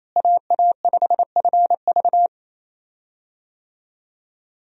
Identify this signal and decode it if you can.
Morse code, heard as a keyed tone, about 28 wpm, 700 Hz. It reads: AA5FV
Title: USA Call Signs - 5 Characters - Repeated 2x - 28wpm